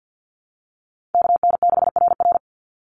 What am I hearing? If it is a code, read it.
Morse code, heard as a keyed tone, about 50 wpm, 700 Hz. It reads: KN6RR
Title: USA Call Signs 50wpm